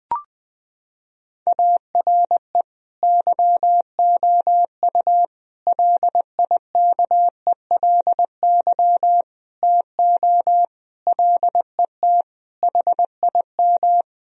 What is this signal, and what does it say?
Morse code, heard as a keyed tone, about 20 wpm, 700 Hz. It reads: ARE YOU LIKELY TO LET HIM
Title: Sentences from Top 1000 Words - Review Part 4 - 20wpm